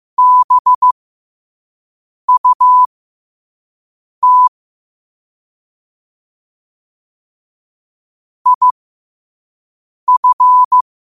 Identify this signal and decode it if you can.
Morse code, heard as a keyed tone, about 15 wpm, spaced as for 5 wpm, 1000 Hz. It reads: BUT IF
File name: qotdfiles / 5wpm